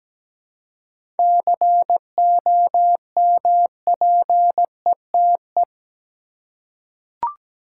Morse code, heard as a keyed tone, about 17 wpm, 700 Hz. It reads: COMPETE
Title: Seven Letter Words - 17wpm